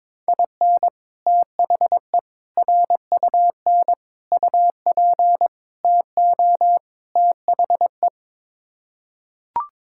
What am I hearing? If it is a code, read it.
Morse code, heard as a keyed tone, about 22 wpm, 700 Hz. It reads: IN THE RUN UP TO THE
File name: practice / 22wpm / Sets of 6 Words - 22wpm